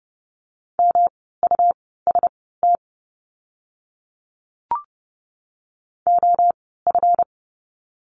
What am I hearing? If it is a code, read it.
Morse code, heard as a keyed tone, about 30 wpm, spaced as for 17 wpm, 700 Hz. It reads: MUST OF
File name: course / 30:10wpm / 237 Lesson 38f - Letter q - Words 30wpm